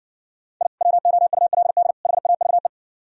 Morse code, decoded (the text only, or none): I WOULD HAVE